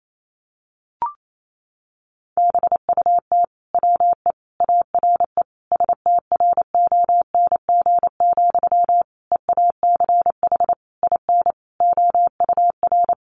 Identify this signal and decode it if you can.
Morse code, heard as a keyed tone, about 28 wpm, 700 Hz. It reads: BUT WE ARE STRONG, EACH IN OUR
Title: Sentences from Top 800 Words 28wpm